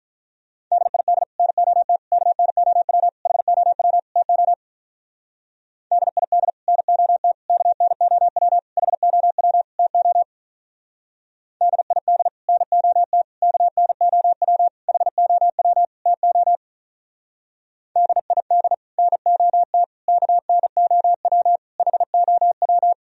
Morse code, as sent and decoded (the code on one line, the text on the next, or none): -.. .. -.. / -. --- - / -.- -. --- .-- / .... --- .-- / - --- / -.. .. -.. / -. --- - / -.- -. --- .-- / .... --- .-- / - --- / -.. .. -.. / -. --- - / -.- -. --- .-- / .... --- .-- / - --- / -.. .. -.. / -. --- - / -.- -. --- .-- / .... --- .--
DID NOT KNOW HOW TO DID NOT KNOW HOW TO DID NOT KNOW HOW TO DID NOT KNOW HOW